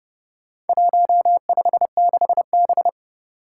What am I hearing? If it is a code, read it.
Morse code, heard as a keyed tone, about 32 wpm, 700 Hz. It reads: 156B
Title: Four Challenging Letter-Number Combinations 30wpm